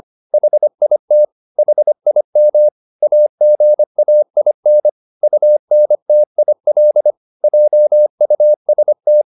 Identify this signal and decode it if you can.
Morse code, heard as a keyed tone, about 25 wpm, 600 Hz. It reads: HIT HIM AGAIN UNTIL JUST